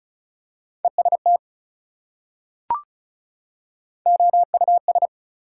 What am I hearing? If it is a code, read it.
Morse code, heard as a keyed tone, about 36 wpm, 700 Hz. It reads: EST OUS
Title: Prefixes and Suffixes 35wpm